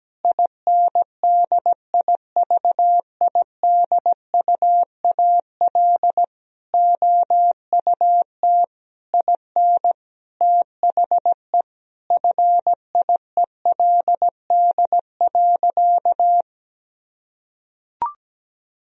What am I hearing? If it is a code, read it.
Morse code, heard as a keyed tone, about 17 wpm, 700 Hz. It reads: INDIVIDUAL OUT IN THE FIELD.
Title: Sentences from Top 1000 Words - Review Part 4 - 17wpm